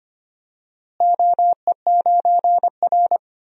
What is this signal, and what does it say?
Morse code, heard as a keyed tone, about 25 wpm, 700 Hz. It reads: OE9R
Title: DXCC-100 Call Signs - 4 Characters - Repeated 3x - 25wpm